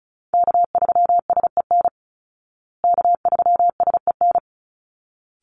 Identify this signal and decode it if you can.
Morse code, heard as a keyed tone, about 34 wpm, 700 Hz. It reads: K3SEN K3SEN
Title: Call Signs Repeated 3x - 35wpm